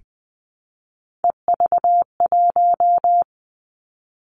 Morse code, heard as a keyed tone, about 20 wpm, 700 Hz. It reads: EV1
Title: Three Challenging Letter-Number Combinations 20wpm